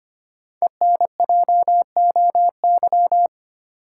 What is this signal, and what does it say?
Morse code, heard as a keyed tone, about 25 wpm, 700 Hz. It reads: ENJOY
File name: 5 Letter Words - First 3 Letters Given - 25wpm